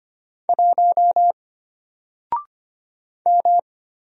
Morse code, heard as a keyed tone, about 25 wpm, 700 Hz. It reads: 1 M
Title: Single Letter-Number Alt 25wpm